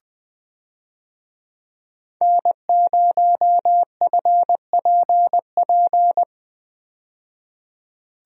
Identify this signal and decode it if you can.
Morse code, heard as a keyed tone, about 20 wpm, 700 Hz. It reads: N0FPP
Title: USA Call Signs - 5 Characters - 20wpm